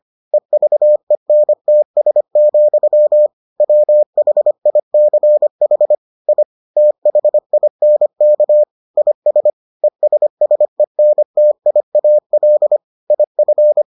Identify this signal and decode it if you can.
Morse code, heard as a keyed tone, about 25 wpm, 600 Hz. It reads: EVENTS, WHICH I THINK IS ESSENTIAL IF